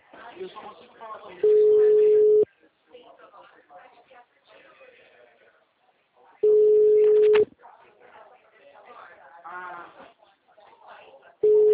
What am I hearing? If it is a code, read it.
Ringback tone (Europe, 425 Hz), with speech.